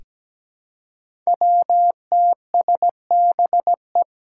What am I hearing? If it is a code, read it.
Morse code, heard as a keyed tone, about 17 wpm, 700 Hz. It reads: WTSBE
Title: Five Challenging Letter-Number Combinations 17wpm